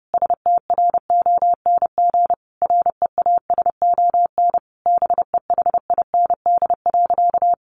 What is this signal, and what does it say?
Morse code, heard as a keyed tone, about 30 wpm, 700 Hz. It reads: STRONG REASON BEHIND.
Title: Sentences from Top 700 Words 30wpm